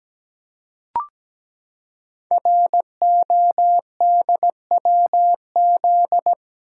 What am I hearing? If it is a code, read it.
Morse code, heard as a keyed tone, about 17 wpm, 700 Hz. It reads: RODWZ
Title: Code Groups - 5 Letters - 17wpm